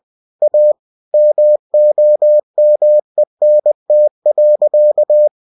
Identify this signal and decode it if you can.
Morse code, heard as a keyed tone, about 20 wpm, 600 Hz. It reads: A MOMENT.
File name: cw-books / winnie-the-pooh / winnie-20wpm-page-0054